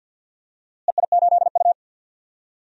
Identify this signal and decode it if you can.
Morse code, heard as a keyed tone, about 50 wpm, 700 Hz. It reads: EI8U
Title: DXCC-100 Call Signs - 4 Characters - Repeated 2x - 50wpm